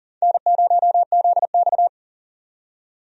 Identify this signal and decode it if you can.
Morse code, heard as a keyed tone, about 40 wpm, 700 Hz. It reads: N0ZX
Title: USA Call Signs - 4 Characters - 40wpm